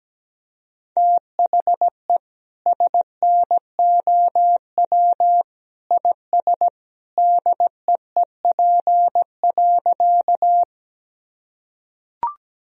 Morse code, heard as a keyed tone, about 17 wpm, 700 Hz. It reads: THE SNOW IS DEEP.
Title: Sentences from Top 1000 Words - Review Part 3 - 17wpm